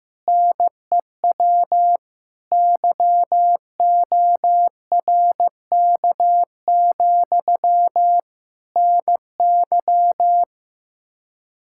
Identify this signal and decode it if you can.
Morse code, heard as a keyed tone, about 15 wpm, 700 Hz. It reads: NEW YORK, NY
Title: K1USN SST QSO Practice 15wpm